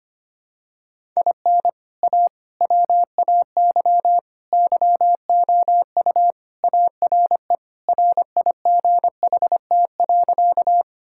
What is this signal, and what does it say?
Morse code, heard as a keyed tone, about 25 wpm, 700 Hz. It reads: IN A WAY YOU ARE RIGHT.
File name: practice / 25wpm / Sentences with 3-Word Phrases - 25wpm